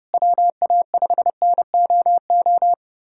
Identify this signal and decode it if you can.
Morse code, heard as a keyed tone, about 30 wpm, 700 Hz. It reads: WA5NOO
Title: USA Callsign DE Callsign - 6 Characters - 30wpm